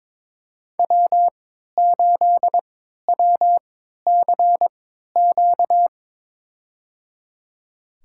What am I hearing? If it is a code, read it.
Morse code, heard as a keyed tone, about 22 wpm, spaced as for 13 wpm, 700 Hz. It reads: W8WCQ
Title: USA Call Signs - 5 Characters - 3x Character Spacing - 22wpm